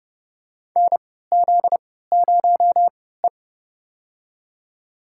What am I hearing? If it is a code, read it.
Morse code, heard as a keyed tone, about 30 wpm, spaced as for 17 wpm, 700 Hz. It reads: NZ0E
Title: USA Call Signs - 4 Characters - 3x Character Spacing - 30wpm